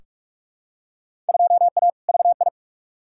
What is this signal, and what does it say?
Morse code, heard as a keyed tone, about 46 wpm, 700 Hz. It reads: 2A VI
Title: Field Day Exchanges Encore - 45wpm